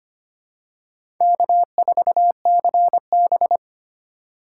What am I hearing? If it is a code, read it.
Morse code, heard as a keyed tone, about 25 wpm, 700 Hz. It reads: K4CB
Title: USA Call Signs - 4 Characters - 4x Character Spacing - 25wpm